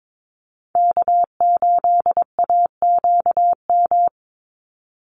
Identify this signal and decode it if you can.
Morse code, heard as a keyed tone, about 22 wpm, 700 Hz. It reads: K8AQM
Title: Call Signs Repeated 3x - 22wpm